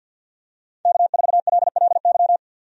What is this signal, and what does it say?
Morse code, heard as a keyed tone, about 52 wpm, 700 Hz. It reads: K4LLY
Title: USA Call Signs - 5 Characters - 50wpm